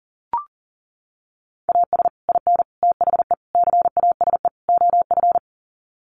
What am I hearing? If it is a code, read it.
Morse code, heard as a keyed tone, about 40 wpm, 700 Hz. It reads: AS IN THE CASE OF
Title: Sets of 5 Words 40wpm